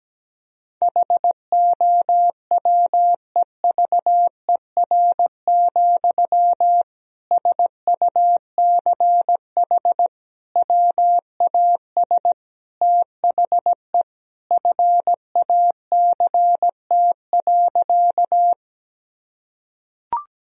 Morse code, heard as a keyed tone, about 17 wpm, 700 Hz. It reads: HOWEVER, SUCH WAS THE FACT.
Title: Sentences from Top 1000 Words - Review Part 3 - 17wpm